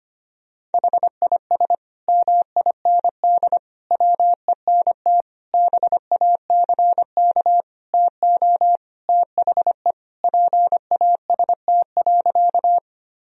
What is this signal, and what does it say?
Morse code, heard as a keyed tone, about 25 wpm, 700 Hz. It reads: HIS MIND WENT BACK TO THE PAST.
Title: Sentences from Top 1000 Words - Review Part 3 - 25wpm